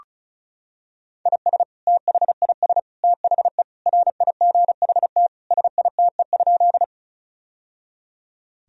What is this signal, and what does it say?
Morse code, heard as a keyed tone, about 34 wpm, 700 Hz. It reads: IS THIS THE RIGHT SITE?
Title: Sentences from Top 1000 Words - Review Part 4 - 35wpm